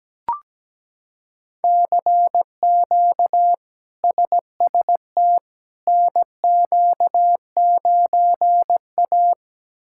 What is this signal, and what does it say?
Morse code, heard as a keyed tone, about 17 wpm, 700 Hz. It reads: CQ SST NQ9A
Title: K1USN SST QSO Practice - 17wpm